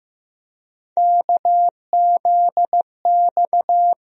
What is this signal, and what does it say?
Morse code, heard as a keyed tone, about 15 wpm, 700 Hz. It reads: KZX
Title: USA Call Sign Postfixes - 3 Letters 15wpm